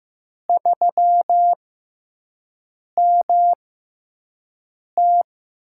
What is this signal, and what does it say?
Morse code, heard as a keyed tone, about 15 wpm, spaced as for 5 wpm, 700 Hz. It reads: 3MT
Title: USA Call Signs - 6x Character Spacing - 15wpm